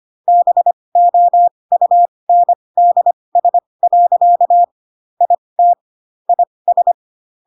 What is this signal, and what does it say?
Morse code, heard as a keyed tone, about 25 wpm, spaced as for 20 wpm, 700 Hz. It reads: BOUNDS. IT IS